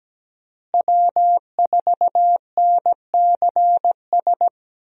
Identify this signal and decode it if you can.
Morse code, heard as a keyed tone, about 17 wpm, 700 Hz. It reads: W4NCS
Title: USA Call Signs - 5 Characters - Repeated 3x - 17wpm